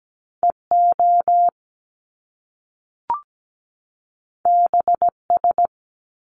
Morse code, heard as a keyed tone, about 17 wpm, 700 Hz. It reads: EO BS
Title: Top Two Letter Combinations in Words 17wpm